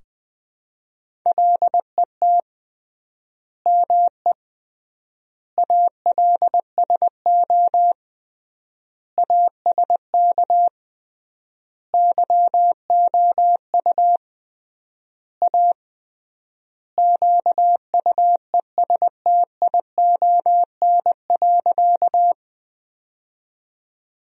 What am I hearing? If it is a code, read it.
Morse code, heard as a keyed tone, about 20 wpm, 700 Hz. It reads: LET ME ALSO ASK YOU A QUESTION.
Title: 3x Word-Spacing - Sentences from Top 400 Words - 20wpm